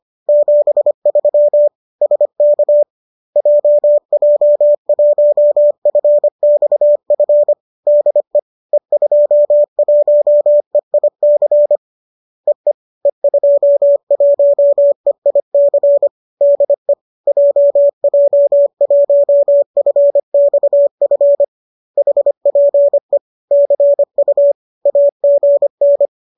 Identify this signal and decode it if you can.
Morse code, heard as a keyed tone, about 25 wpm, 570 Hz. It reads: 73 SK JJ1FXF DE E21EIC EE E21EIC DE JJ1FXF HPE CU AGN
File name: qso208 25